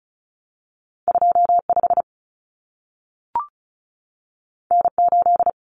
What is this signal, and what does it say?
Morse code, heard as a keyed tone, about 36 wpm, 700 Hz. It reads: 25 N8
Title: Common Cut Numbers - 2-Digit 35wpm